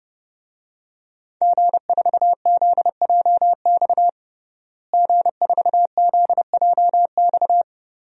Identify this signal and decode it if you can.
Morse code, heard as a keyed tone, about 30 wpm, 700 Hz. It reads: G4ZJX G4ZJX
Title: DXCC-100 Call Signs - 5 Characters - Repeated 3x - 30wpm